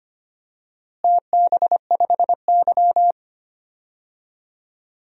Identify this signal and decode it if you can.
Morse code, heard as a keyed tone, about 25 wpm, 700 Hz. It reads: TB5Y
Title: DXCC-100 Call Signs - 4 Characters - Repeated 3x - 25wpm